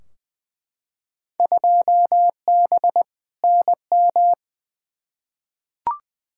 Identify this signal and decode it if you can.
Morse code, heard as a keyed tone, about 21 wpm, 700 Hz. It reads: 2B NM
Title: Field Day Exchanges Encore - 20wpm